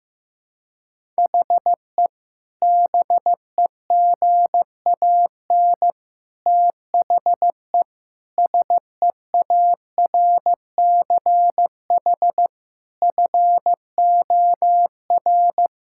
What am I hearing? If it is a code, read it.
Morse code, heard as a keyed tone, about 15 wpm, 700 Hz. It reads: HE BEGAN THE SEARCH FOR
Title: Sentences from Top 1000 Words - Review Part 5 - 15wpm